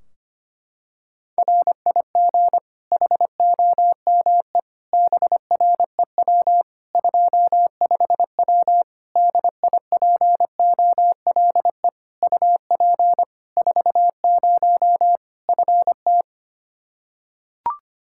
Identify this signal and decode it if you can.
Morse code, heard as a keyed tone, about 25 wpm, 700 Hz. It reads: RIG HOME BREW 25W DIPOLE UP 40 FT
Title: K1USN SST QSO Practice 25wpm